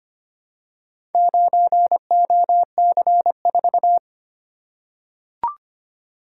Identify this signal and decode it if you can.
Morse code, heard as a keyed tone, about 25 wpm, 700 Hz. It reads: 9OC4
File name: practice / 25wpm / Four Letter-Number Combinations - 25wpm